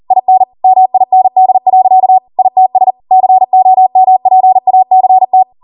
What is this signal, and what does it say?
Morse code, heard as a keyed tone, about 40 wpm, 750 Hz. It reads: IN MIND. ITS COMPACT
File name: Morse / 40WPM